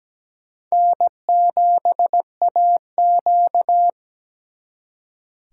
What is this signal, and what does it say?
Morse code, heard as a keyed tone, about 20 wpm, 700 Hz. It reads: N7AQ
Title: Speed Racing - USA 1x2 Call Signs 17wpm